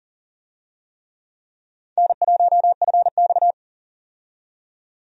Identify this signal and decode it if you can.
Morse code, heard as a keyed tone, about 40 wpm, 700 Hz. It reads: N1FX
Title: USA Call Signs - 4 Characters - 40wpm